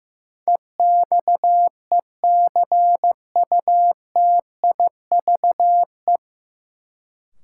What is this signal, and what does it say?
Morse code, heard as a keyed tone, about 15 wpm, 700 Hz. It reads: EXECUTIVE
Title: Sets of 5 Words Encore - New Word Drill - 15wpm